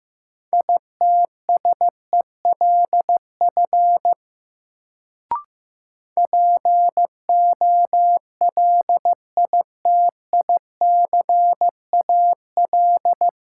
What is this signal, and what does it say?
Morse code, heard as a keyed tone, about 15 wpm, 700 Hz. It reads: ITSELF POLITICAL
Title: Sentences from Top 800 Words - New Word Drill - 15wpm